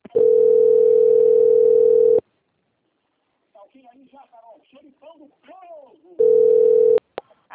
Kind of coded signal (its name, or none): telephone call progress tone